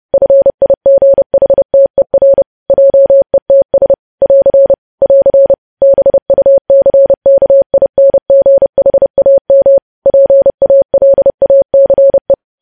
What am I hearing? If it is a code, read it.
Morse code, heard as a keyed tone, about 30 wpm, 550 Hz. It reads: FIGHTER JETS <AR> <AR> BUCKINGHAM PALACE
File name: podcast / News headlines-1x-30x30